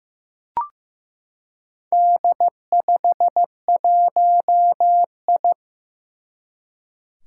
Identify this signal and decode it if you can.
Morse code, heard as a keyed tone, about 15 wpm, 700 Hz. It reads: D51I
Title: Four Challenging Letter-Number Combinations 15wpm